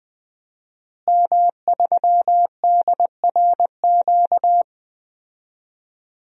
Morse code, heard as a keyed tone, about 20 wpm, 700 Hz. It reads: M3DRQ
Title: DXCC-100 Call Signs - 5 Characters - Repeated 2x - 20wpm